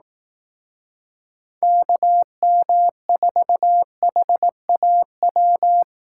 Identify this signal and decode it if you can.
Morse code, heard as a keyed tone, about 18 wpm, 700 Hz. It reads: KM4HAW
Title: Speed Racing - USA Call Signs 15wpm